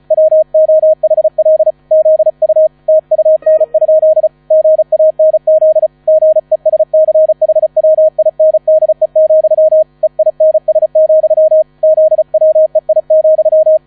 Morse code, heard as a keyed tone, about 35 wpm, 630 Hz, with radio noise. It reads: WOHL ZU TUN? GANZ GESCHWINDE, EINS, ZWEI,
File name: 35wpm